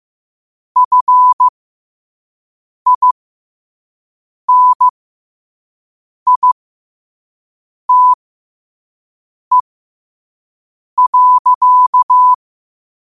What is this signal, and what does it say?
Morse code, heard as a keyed tone, about 15 wpm, spaced as for 5 wpm, 1000 Hz. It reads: FINITE.
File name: qotdfiles / 5wpm